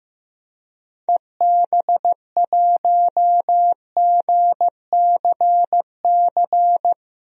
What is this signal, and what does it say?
Morse code, heard as a keyed tone, about 15 wpm, 700 Hz. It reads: EB1GCC
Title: DXCC-100 Call Signs - 6 Characters - 15wpm